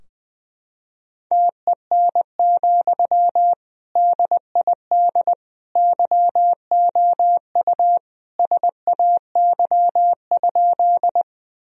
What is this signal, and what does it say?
Morse code, heard as a keyed tone, about 19 wpm, 700 Hz. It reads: TEN, DID YOU SAY?
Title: Sentences from Top 1000 Words - Review Part 2 - 20wpm